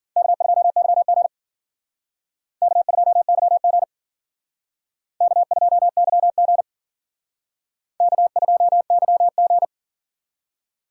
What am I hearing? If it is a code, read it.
Morse code, heard as a keyed tone, about 52 wpm, 700 Hz. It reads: K2YG K2YG K2YG K2YG